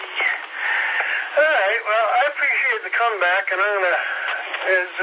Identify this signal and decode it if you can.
Emergency Alert System (SAME) tones, heard as harsh buzzing bursts, with speech. It is heard from 0.7 to 4.5 s.